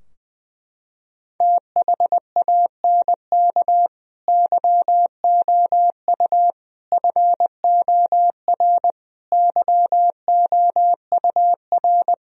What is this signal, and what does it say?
Morse code, heard as a keyed tone, about 20 wpm, 700 Hz. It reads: THANK YOU FOR YOUR
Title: Sentences from Top 1000 Words - Review Part 3 - 20wpm